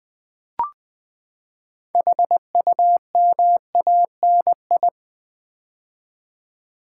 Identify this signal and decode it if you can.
Morse code, heard as a keyed tone, about 20 wpm, 700 Hz. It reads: HUMANI
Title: Build A Word - 6 Letters At A Time - 12-Letter Words - 20wpm